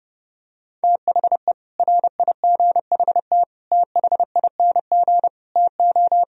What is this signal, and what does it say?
Morse code, heard as a keyed tone, about 29 wpm, 700 Hz. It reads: THE RIGHT THING TO